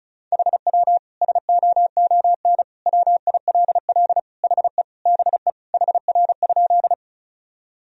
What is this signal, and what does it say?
Morse code, heard as a keyed tone, about 36 wpm, 700 Hz. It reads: HW SOON WILL HE BE HR?
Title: Sentences from Top 400 Words and Abbreviations 35wpm